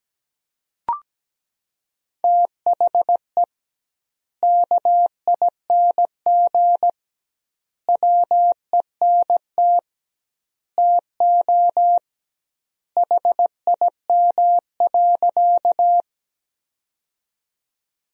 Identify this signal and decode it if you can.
Morse code, heard as a keyed tone, about 17 wpm, 700 Hz. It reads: THE KING WENT TO HIM.
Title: 2x Word-Spacing - Sentences from Top 400 Words - 17wpm